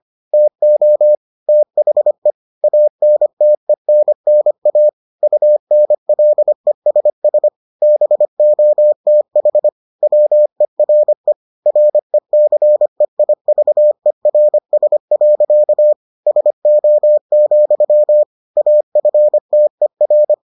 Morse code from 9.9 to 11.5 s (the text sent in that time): WERE